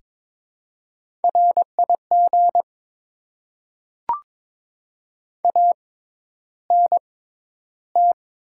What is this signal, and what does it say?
Morse code, heard as a keyed tone, about 22 wpm, spaced as for 11 wpm, 700 Hz. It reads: RIG ANT
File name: Top 20 QSO Elements - 6x Character Spacing - 22wpm